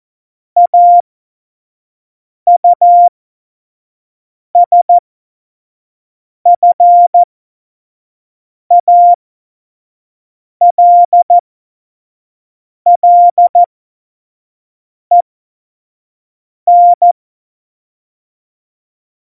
Morse code, heard as a keyed tone, about 14 wpm, spaced as for 5 wpm, 700 Hz. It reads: AUSFALLEN
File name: Morsekurs / Witze / Witz 11. September 2018, 25 F70